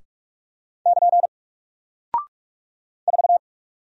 Morse code, heard as a keyed tone, about 44 wpm, 700 Hz. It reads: <KN> 4
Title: Single Character (Letter:Number:Punctuation:Prosign) Alt 45wpm